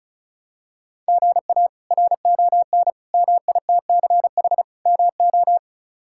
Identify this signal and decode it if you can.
Morse code, heard as a keyed tone, about 36 wpm, 700 Hz. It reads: GA RON MITCH MO
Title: K1USN SST QSO Practice - 35wpm